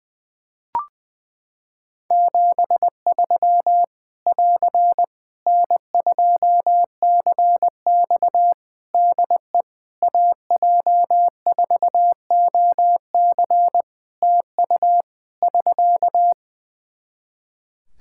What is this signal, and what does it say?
Morse code, heard as a keyed tone, about 20 wpm, 700 Hz. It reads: 73 <AR> N2CX DE AJ4OC TU <SK>
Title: Practice QSO - Exchange 4 - Ending - 20wpm